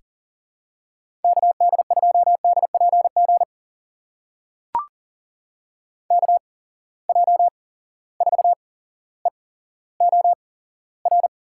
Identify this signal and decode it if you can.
Morse code, heard as a keyed tone, about 40 wpm, spaced as for 17 wpm, 700 Hz. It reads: KD2DPG KJ4EOR
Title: USA Call Signs - 6 Characters - 8x Character Spacing - 40wpm